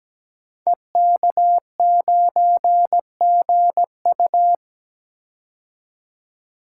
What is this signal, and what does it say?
Morse code, heard as a keyed tone, about 17 wpm, 700 Hz. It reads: EK9GU
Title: Code Groups - 5 Letter:Numbers - 17wpm